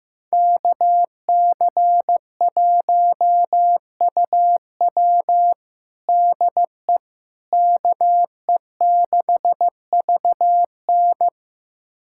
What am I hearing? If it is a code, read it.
Morse code, heard as a keyed tone, about 15 wpm, 700 Hz. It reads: KC1UW DE KE6VN
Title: USA Callsign DE Callsign - 5 Characters - 15wpm